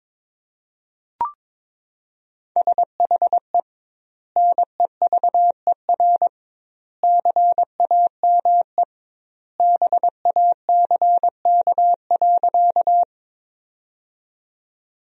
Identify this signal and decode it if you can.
Morse code, heard as a keyed tone, about 22 wpm, 700 Hz. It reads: SHE NEVER CAME BACK.
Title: 2x Word-Spacing - Sentences from Top 300 Words - 22wpm